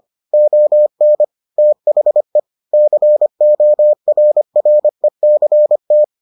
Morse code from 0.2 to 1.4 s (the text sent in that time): ON